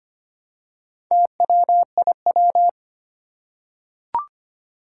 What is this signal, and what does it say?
Morse code, heard as a keyed tone, about 25 wpm, 700 Hz. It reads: TWIW
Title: Four Challenging Letter-Number Combinations 25wpm